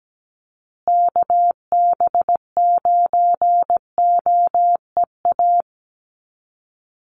Morse code, heard as a keyed tone, about 17 wpm, 700 Hz. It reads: KB9OEA
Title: USA Call Signs Repeated 2x 17wpm